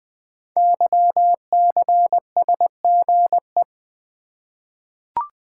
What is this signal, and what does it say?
Morse code, heard as a keyed tone, about 20 wpm, 700 Hz. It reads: YCSGE
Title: Code Groups - 5 Letters - 20wpm